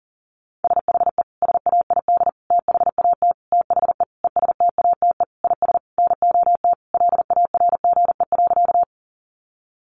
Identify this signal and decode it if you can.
Morse code, heard as a keyed tone, about 40 wpm, 700 Hz. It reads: SHE SAID THAT THE ESTATE IS NOT LARGE.
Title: Sentences from Top 1200 Words 40wpm